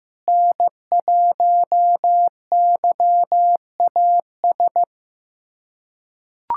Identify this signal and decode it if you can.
Morse code, heard as a keyed tone, about 15 wpm, 700 Hz. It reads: N1YAS
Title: USA Callsign DE Callsign - 5 Characters - 15wpm